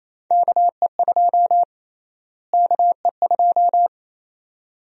Morse code, heard as a keyed tone, about 27 wpm, 700 Hz. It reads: KE2 KE2
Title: USA Call Sign Prefixes - Repeated 2x - 28wpm